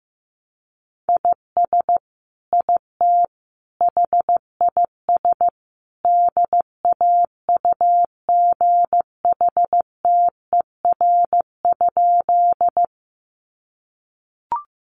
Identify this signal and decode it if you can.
Morse code, heard as a keyed tone, about 15 wpm, 700 Hz. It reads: IS IT HIS DAUGHTER?
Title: Sentences from Top 1500 Words 15wpm